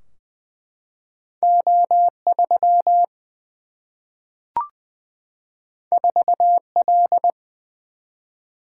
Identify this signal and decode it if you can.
Morse code, heard as a keyed tone, about 20 wpm, 700 Hz. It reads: O 3 4 L
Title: Single-Letter Single-Number 20wpm